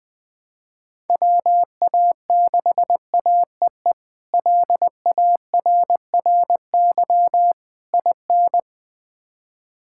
Morse code, heard as a keyed tone, about 21 wpm, 700 Hz. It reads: WA6AEE LARRY IN
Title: K1USN SST QSO Practice - 20wpm